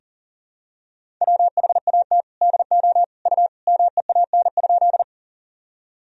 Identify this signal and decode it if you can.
Morse code, heard as a keyed tone, about 40 wpm, 700 Hz. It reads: WHAT DO U MEAN?
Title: Sentences from Top 200 Words and Abbreviations 40wpm